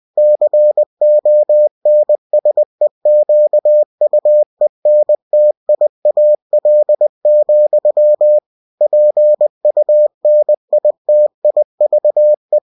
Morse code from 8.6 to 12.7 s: PUNITIVE